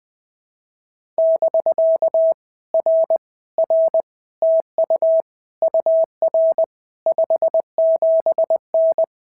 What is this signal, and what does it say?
Morse code, heard as a keyed tone, about 20 wpm, 650 Hz. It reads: <BK> R R TU UR 57N
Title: POTA QSO Practice - 20wpm